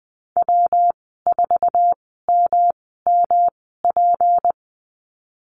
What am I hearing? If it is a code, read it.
Morse code, heard as a keyed tone, about 20 wpm, spaced as for 15 wpm, 700 Hz. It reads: W4MMP
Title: 238 Lesson 38g - Letter q - Call Signs 20wpm